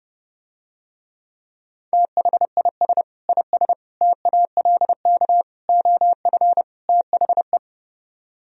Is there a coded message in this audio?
Yes: Morse code from 1.9 to 7.6 s, heard as a keyed tone, about 30 wpm, 700 Hz. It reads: THIS IS TALK OF THE